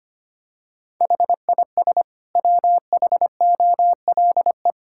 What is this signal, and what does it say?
Morse code, heard as a keyed tone, about 25 wpm, 700 Hz. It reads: HIS WHOLE
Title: Sentences from Top 1000 Words - Review Part 6 - 25wpm